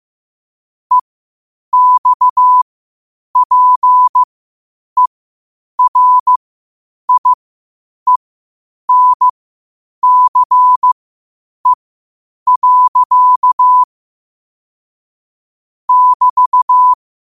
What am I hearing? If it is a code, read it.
Morse code, heard as a keyed tone, about 15 wpm, spaced as for 8 wpm, 1000 Hz. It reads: EXPERIENCE. <BT>
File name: qotdfiles / 8wpm